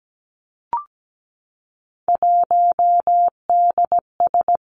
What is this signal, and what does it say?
Morse code, heard as a keyed tone, about 17 wpm, 700 Hz. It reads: 1DS
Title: Three Challenging Letter-Number Combinations 17wpm